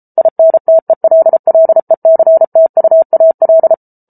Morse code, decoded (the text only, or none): INTELLECTUAL